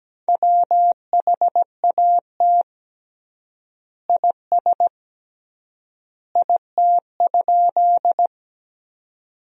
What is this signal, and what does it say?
Morse code, heard as a keyed tone, about 17 wpm, 700 Hz. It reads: WHAT IS IT?
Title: 3x Word-Spacing - Sentences from Top 100 Words - 17wpm